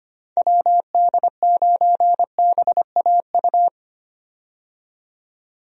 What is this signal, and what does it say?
Morse code, heard as a keyed tone, about 25 wpm, 700 Hz. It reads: WD9BAU
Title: USA Call Signs - 6 Characters - 25wpm